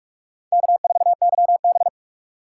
Morse code, heard as a keyed tone, about 41 wpm, 700 Hz. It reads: K4YB
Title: USA Call Signs - 2x Character Spacing - 45wpm